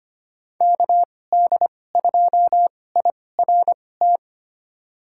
Morse code, heard as a keyed tone, about 25 wpm, spaced as for 18 wpm, 700 Hz. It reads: KD2IRT
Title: USA Call Signs - 2x Character Spacing - 25wpm